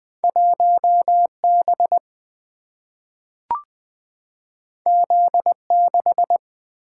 Morse code, heard as a keyed tone, about 20 wpm, 700 Hz. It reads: 1B Z6
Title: Two Letter-Number Combinations - 20wpm